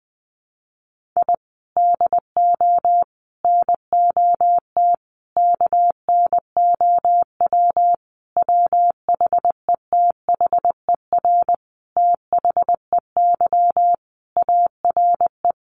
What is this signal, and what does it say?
Morse code, heard as a keyed tone, about 20 wpm, 700 Hz. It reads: I DO NOT KNOW WHETHER THEY ARE
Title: Sentences from Top 1500 Words 20wpm